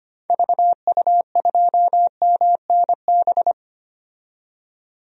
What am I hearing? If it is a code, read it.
Morse code, heard as a keyed tone, about 25 wpm, 700 Hz. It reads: VU2MNB
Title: DXCC-100 Call Signs - 6 Characters - 25wpm